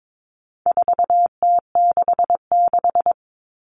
Morse code, heard as a keyed tone, about 22 wpm, 700 Hz. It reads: 4T66
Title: All Cut Numbers - 4-Digit 22wpm